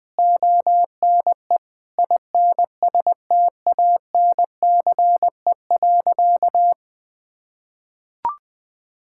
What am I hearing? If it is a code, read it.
Morse code, heard as a keyed tone, about 20 wpm, 700 Hz. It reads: ONE INSTANCE.
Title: Sentences from Top 1000 Words - Review Part 5 - 20wpm